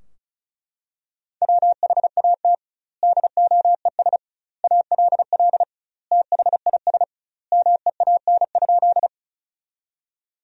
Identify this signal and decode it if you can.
Morse code, heard as a keyed tone, about 36 wpm, 700 Hz. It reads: WHAT DOES ALL THIS MEAN?
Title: 2x Word-Spacing - Sentences from Top 200 Words - 35wpm